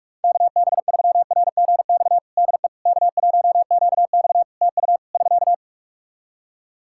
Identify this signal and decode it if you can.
Morse code, heard as a keyed tone, about 46 wpm, 700 Hz. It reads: KB3RGX DE K1QX TU <SK>
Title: Practice QSO - Exchange 4 - Ending - 45wpm